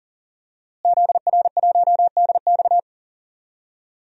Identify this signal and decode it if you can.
Morse code, heard as a keyed tone, about 40 wpm, 700 Hz. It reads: ZR1DX
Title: DXCC-100 Call Signs - 5 Characters - Repeated 3x - 40wpm